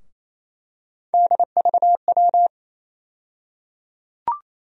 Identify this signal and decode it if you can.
Morse code, heard as a keyed tone, about 27 wpm, 700 Hz. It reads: DVW
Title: USA Call Sign Postfixes - 3 Letters 28wpm